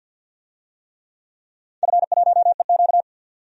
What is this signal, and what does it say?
Morse code, heard as a keyed tone, about 53 wpm, 700 Hz. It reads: F1EQ
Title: DXCC-100 Call Signs - 4 Characters - 50wpm